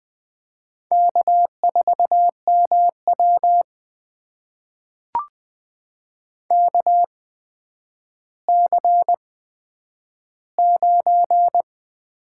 Morse code, heard as a keyed tone, about 21 wpm, spaced as for 10 wpm, 700 Hz. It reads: K4MW KC9
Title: USA Call Signs - 4 Characters - 8x Character Spacing - 20wpm